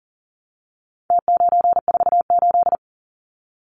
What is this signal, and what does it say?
Morse code, heard as a keyed tone, about 40 wpm, 700 Hz. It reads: T948
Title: Numbers - Contest Style - 4-Digit 40wpm